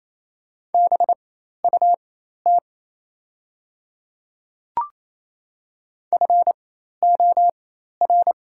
Morse code, heard as a keyed tone, about 27 wpm, spaced as for 13 wpm, 700 Hz. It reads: BUT FOR
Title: 4x Character Spacing - Top 300 Words - 28wpm